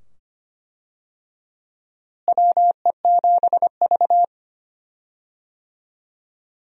Morse code, heard as a keyed tone, about 25 wpm, 700 Hz. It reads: WE7V
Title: USA Call Signs - 4 Characters - Repeated 2x - 25wpm